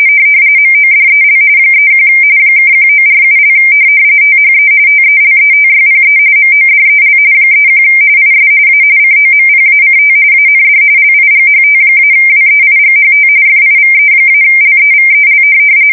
RTTY, about 45.45 baud. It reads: MODEL 31 (1947) THE TELETYPE MODEL 31 IS A COMPACT AND LIGHT-WEIGHT FIVE-LEVEL
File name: fsk170 / teletype-corp